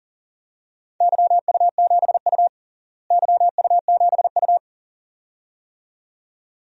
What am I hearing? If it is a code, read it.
Morse code, heard as a keyed tone, about 40 wpm, 700 Hz. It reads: YU7U YU7U
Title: DXCC-100 Call Signs - 4 Characters - Repeated 2x - 40wpm